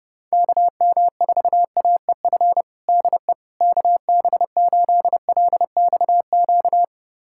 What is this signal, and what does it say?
Morse code, heard as a keyed tone, about 32 wpm, 700 Hz. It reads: KM4AEF DE KB8LXQ
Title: USA Callsign DE Callsign - 6 Characters - 30wpm